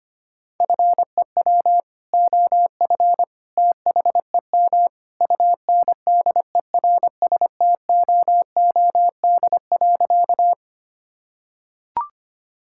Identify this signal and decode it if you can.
Morse code, heard as a keyed tone, about 25 wpm, 700 Hz. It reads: FEW OF THEM UNDERSTOOD.
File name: practice / 25wpm / Sentences with 2-Word Phrases - 25wpm